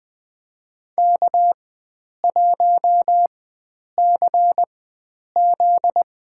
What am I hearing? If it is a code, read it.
Morse code, heard as a keyed tone, about 20 wpm, spaced as for 9 wpm, 700 Hz. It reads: K1CZ
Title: USA Call Signs - 4x Character Spacing - 20wpm